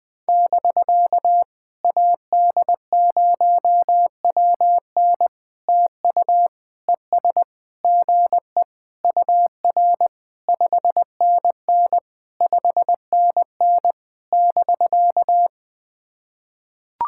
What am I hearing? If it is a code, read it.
Morse code, heard as a keyed tone, about 20 wpm, 700 Hz. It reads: <BK> AD0WN TU ES GE UR 5NN 5NN <BK>
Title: POTA QSO Practice - 20wpm